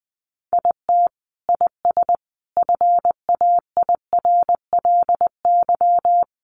Morse code, heard as a keyed tone, about 20 wpm, 700 Hz. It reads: IT IS FAIRLY
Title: Sentences from Top 1100 Words 20wpm